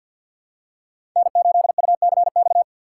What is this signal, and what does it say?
Morse code, heard as a keyed tone, about 50 wpm, 690 Hz. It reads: N8UCX
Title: USA Call Signs - 4x Character Spacing - 50wpm